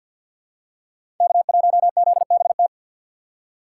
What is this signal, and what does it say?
Morse code, heard as a keyed tone, about 53 wpm, 700 Hz. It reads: K1ZBT
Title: USA Call Signs - 5 Characters - 6x Character Spacing - 50wpm